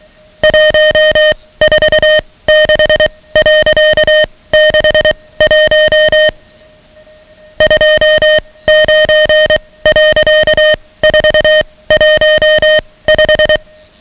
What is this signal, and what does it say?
Morse code, heard as a keyed tone, about 23 wpm, spaced as for 17 wpm, 620 Hz, with noise. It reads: 146.61 29.415
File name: ham radio / CodeQuick / Code Quick 10